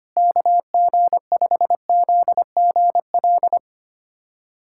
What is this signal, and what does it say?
Morse code, heard as a keyed tone, about 25 wpm, 700 Hz. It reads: KG5ZGL
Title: USA Callsign DE Callsign - 6 Characters - 25wpm